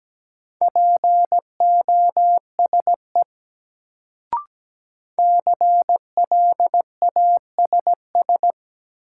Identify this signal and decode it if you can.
Morse code, heard as a keyed tone, about 17 wpm, 700 Hz. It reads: POSE CLASS
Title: Sentences from Top 1000 Words - Review Part 2 - New Word Drill - 17wpm